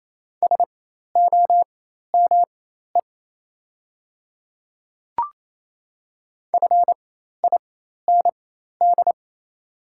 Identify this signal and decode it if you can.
Morse code, heard as a keyed tone, about 27 wpm, spaced as for 13 wpm, 700 Hz. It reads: SOME FIND
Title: 4x Character Spacing - Top 100 Words - 28wpm